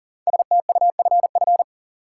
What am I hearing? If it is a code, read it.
Morse code, heard as a keyed tone, about 40 wpm, 700 Hz. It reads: STUFF
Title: Sentences from Top 1400 Words - New Word Drill - 40wpm